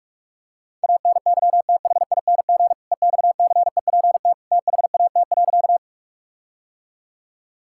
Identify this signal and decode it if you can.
Morse code, heard as a keyed tone, about 46 wpm, 700 Hz. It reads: ANYTHING EXCEPT THAT.
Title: Sentences from Top 1600 Words 45wpm